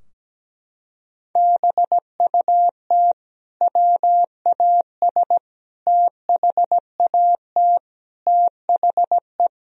Morse code, heard as a keyed tone, about 17 wpm, 700 Hz. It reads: BUT WAS THAT THE
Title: Sentences from Top 1000 Words - Review Part 2 - 17wpm